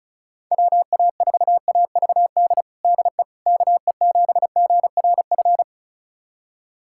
Morse code, heard as a keyed tone, about 36 wpm, 690 Hz. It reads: WA4AVD DE KE7GRF
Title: USA Callsign DE Callsign 35wpm